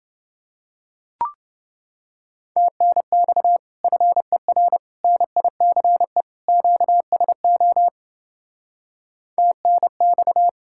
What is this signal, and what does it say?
Morse code, heard as a keyed tone, about 29 wpm, 700 Hz. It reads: TNX FER NICE QSO TNX